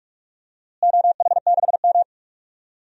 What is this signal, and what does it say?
Morse code, heard as a keyed tone, about 46 wpm, 700 Hz. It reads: OH6M
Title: DXCC-100 Call Signs - 4 Characters - Repeated 2x - 45wpm